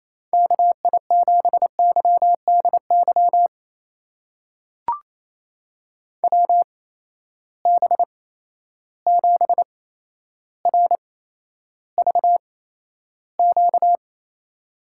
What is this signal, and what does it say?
Morse code, heard as a keyed tone, about 28 wpm, spaced as for 12 wpm, 700 Hz. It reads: KI7YDY WB7RVQ
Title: USA Call Signs - 6 Characters - 8x Character Spacing - 28wpm